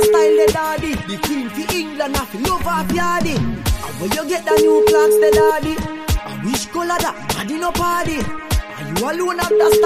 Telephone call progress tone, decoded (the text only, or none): ringback tone